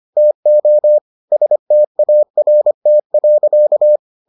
Morse code, heard as a keyed tone, about 25 wpm, 600 Hz. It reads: TO START.